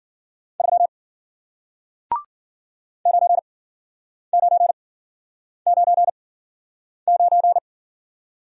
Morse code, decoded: 3 9999

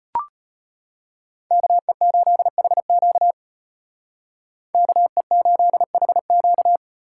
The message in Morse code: KE8HQ KE8HQ

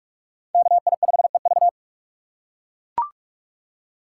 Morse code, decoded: KI5EV

